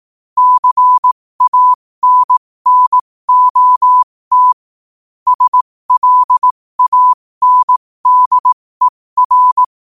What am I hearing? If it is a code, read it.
Morse code, heard as a keyed tone, about 18 wpm, spaced as for 15 wpm, 1000 Hz. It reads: CANNOT SLANDER
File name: qotdfiles / 15wpm